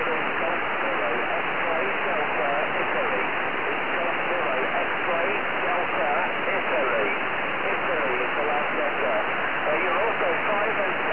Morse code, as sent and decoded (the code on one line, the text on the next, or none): none